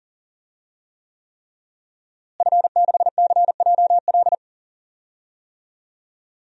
Morse code, heard as a keyed tone, about 40 wpm, 700 Hz. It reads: F6CJL
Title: DXCC-100 Call Signs - 5 Characters - Repeated 3x - 40wpm